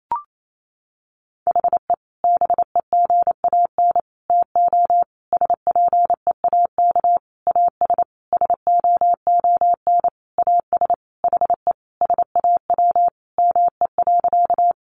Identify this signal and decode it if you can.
Morse code, heard as a keyed tone, about 28 wpm, 700 Hz. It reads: HE BEGAN TO SPEAK AS SOON AS HE SAW ME.
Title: Sentences from Top 1100 Words 28wpm